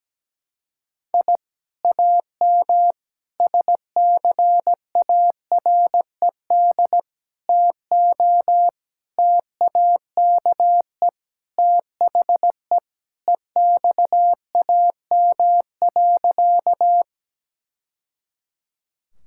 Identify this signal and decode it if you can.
Morse code, heard as a keyed tone, about 17 wpm, 700 Hz. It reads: I AM SCARED TO TAKE THE EXAM.
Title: Sentences from Top 1000 Words - Review Part 5 - 17wpm